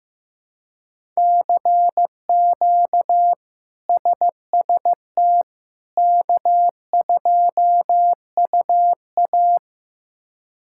Morse code, heard as a keyed tone, about 15 wpm, 700 Hz. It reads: CQ SST K2UA
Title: K1USN SST QSO Practice - 15wpm